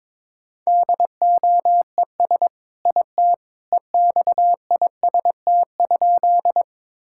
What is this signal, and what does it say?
Morse code, heard as a keyed tone, about 22 wpm, 700 Hz. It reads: DOES IT EXIST?
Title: Sentences from Top 1000 Words - Review Part 5 - 22wpm